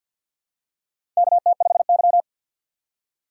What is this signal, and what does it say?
Morse code, heard as a keyed tone, about 52 wpm, 700 Hz. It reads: KT5Y
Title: USA Call Signs - 6x Character Spacing - 50wpm